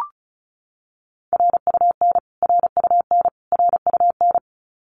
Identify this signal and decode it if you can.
Morse code, heard as a keyed tone, about 34 wpm, 700 Hz. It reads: RUN RUN RUN
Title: Top 500 Words Repeated 35wpm